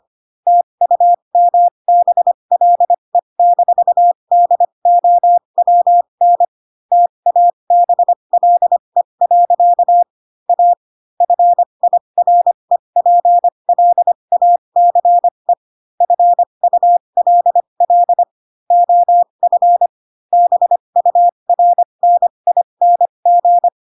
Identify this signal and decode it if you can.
Morse code, heard as a keyed tone, about 25 wpm, spaced as for 22 wpm, 700 Hz. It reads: TUMBLE-DOWN TABLE. A FIREPLACE FULL OF BURNING
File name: Pinocchio / Pinocchio ch3 18 25 ik1hge